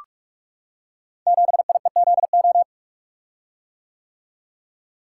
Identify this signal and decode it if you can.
Morse code, heard as a keyed tone, about 44 wpm, 700 Hz. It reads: 7IEZO
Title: Code Groups - 5 Letter:Numbers - 45wpm